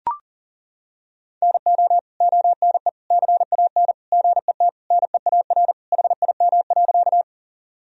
Morse code, heard as a keyed tone, about 40 wpm, 700 Hz. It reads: NO ONE CAN GET NEAR HIM.
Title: Sentences from Top 1000 Words - Review Part 1 - 40wpm